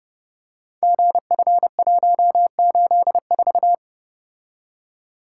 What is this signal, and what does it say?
Morse code, heard as a keyed tone, about 30 wpm, 700 Hz. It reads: GF184
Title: Code Groups - 5 Letter:Numbers - 30wpm